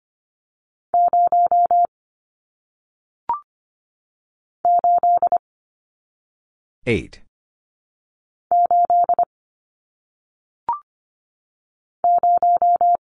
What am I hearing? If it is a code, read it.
Morse code, heard as a keyed tone, about 25 wpm, 700 Hz. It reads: 0 8 8 0
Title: Numbers - 1-Digit 25wpm